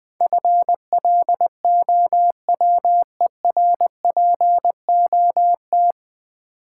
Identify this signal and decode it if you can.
Morse code, heard as a keyed tone, about 20 wpm, 700 Hz. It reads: FLOWERPOT
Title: Easy Words - Speed-Racing with Hints - 20wpm